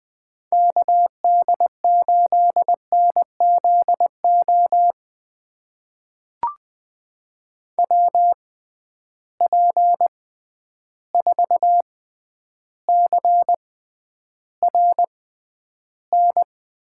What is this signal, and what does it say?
Morse code, heard as a keyed tone, about 20 wpm, spaced as for 10 wpm, 700 Hz. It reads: KD8NZO WP4CRN
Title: USA Call Signs - 6 Characters - 6x Character Spacing - 20wpm